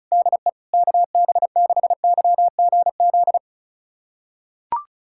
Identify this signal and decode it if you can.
Morse code, heard as a keyed tone, about 34 wpm, 690 Hz. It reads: DE KB6YGZ
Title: USA Callsign DE Callsign - 6 Characters - 35wpm